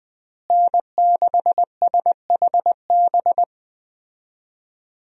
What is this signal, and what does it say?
Morse code, heard as a keyed tone, about 20 wpm, 700 Hz. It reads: N6SHB
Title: USA Call Signs - 5 Characters - Repeated 2x - 20wpm